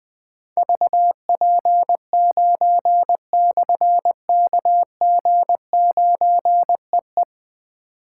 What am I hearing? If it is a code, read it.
Morse code, heard as a keyed tone, about 20 wpm, 700 Hz. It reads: VP9/KG9EE
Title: Difficult Call Signs 20wpm